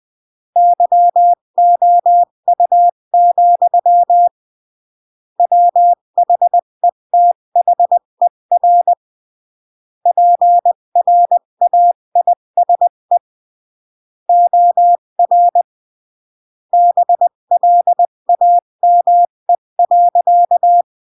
Morse code, heard as a keyed tone, about 20 wpm, 700 Hz. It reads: YOU, WHETHER PRAISE OR BLAME.
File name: CW / jodorowsky / 82 commandments 20-15